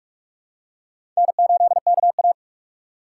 Morse code, heard as a keyed tone, about 46 wpm, 700 Hz. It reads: N8KA
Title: USA Call Signs - 4 Characters - 4x Character Spacing - 45wpm